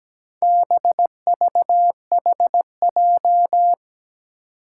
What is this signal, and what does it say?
Morse code, heard as a keyed tone, about 17 wpm, 700 Hz. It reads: BVHJ
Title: Four Challenging Letter-Number Combinations 17wpm